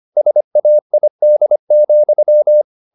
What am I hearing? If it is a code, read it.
Morse code, heard as a keyed tone, about 25 wpm, 600 Hz. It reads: SAID,
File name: the old man and the sea part 1 25wpm